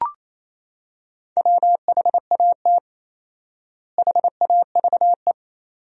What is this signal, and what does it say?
Morse code, heard as a keyed tone, about 28 wpm, 700 Hz. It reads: WHAT HAVE
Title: 4x Word-Spacing - Sentences from Top 500 Words - 28wpm